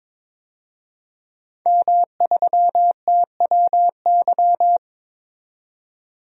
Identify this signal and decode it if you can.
Morse code, heard as a keyed tone, about 22 wpm, 700 Hz. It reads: M3TWY
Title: DXCC-100 Call Signs - 5 Characters - 22wpm